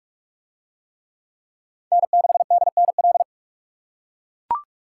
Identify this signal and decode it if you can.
Morse code, heard as a keyed tone, about 44 wpm, 700 Hz. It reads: N6DNL